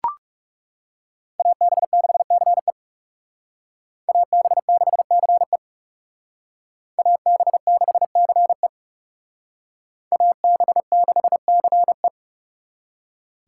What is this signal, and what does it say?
Morse code, heard as a keyed tone, about 38 wpm, 700 Hz. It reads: AB6CE AB6CE AB6CE AB6CE